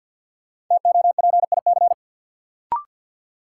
Morse code, heard as a keyed tone, about 53 wpm, 700 Hz. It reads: TOPIC
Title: Sentences from Top 1200 Words - New Word Drill - 50wpm